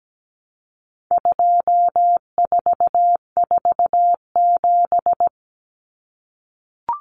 Morse code, heard as a keyed tone, about 17 wpm, 700 Hz. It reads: 2447
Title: Numbers - 4-Digit 17wpm